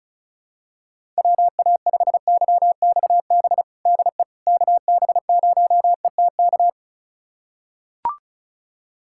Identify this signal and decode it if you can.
Morse code, heard as a keyed tone, about 34 wpm, 700 Hz. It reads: WA5YXB DE KB0ETK
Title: USA Callsign DE Callsign - 6 Characters - 35wpm